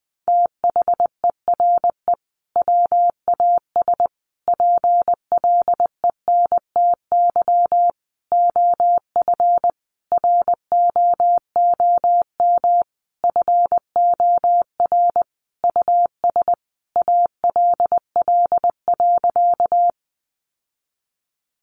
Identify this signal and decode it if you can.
Morse code, heard as a keyed tone, about 20 wpm, 700 Hz. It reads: THERE WAS PLENTY OF ROOM FOR US ALL.
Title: Sentences from Top 1300 Words 20wpm